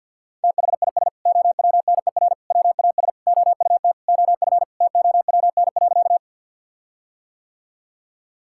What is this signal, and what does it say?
Morse code, heard as a keyed tone, about 50 wpm, 700 Hz. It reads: THIS OWNER WAS OUT OF TOWN.
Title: Sentences from Top 1600 Words 50wpm